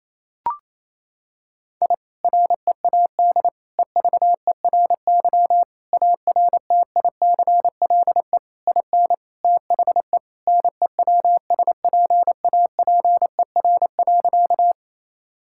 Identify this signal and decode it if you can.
Morse code, heard as a keyed tone, about 28 wpm, 700 Hz. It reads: I READ EVERY ARTICLE IN THE NEWSPAPER.
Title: Sentences from Top 1000 Words - Review Part 5 - 28wpm